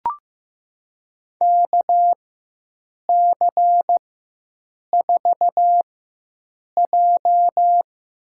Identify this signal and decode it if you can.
Morse code, heard as a keyed tone, about 15 wpm, 700 Hz. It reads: K C 4 J
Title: USA Call Signs - 6 Characters - 4x Character Spacing - 15wpm